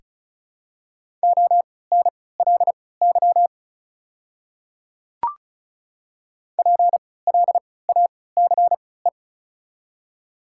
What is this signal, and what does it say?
Morse code, heard as a keyed tone, about 36 wpm, spaced as for 20 wpm, 700 Hz. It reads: ONLY PLACE
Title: 3x Character Spacing - Top 200 Words - 35wpm